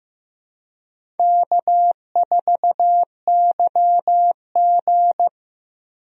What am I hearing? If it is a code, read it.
Morse code, heard as a keyed tone, about 15 wpm, 700 Hz. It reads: K4YG
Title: USA Call Signs - 4 Characters - Repeated 2x - 15wpm